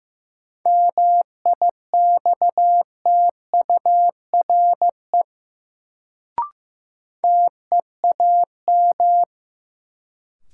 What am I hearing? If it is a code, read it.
Morse code, heard as a keyed tone, about 15 wpm, 700 Hz. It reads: MIXTURE TEAM
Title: Sentences from Top 1500 Words - New Word Drill Alt - 15wpm